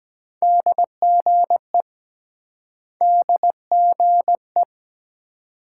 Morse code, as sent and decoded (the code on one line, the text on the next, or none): -.. --. / -.. --. .
DGE DGE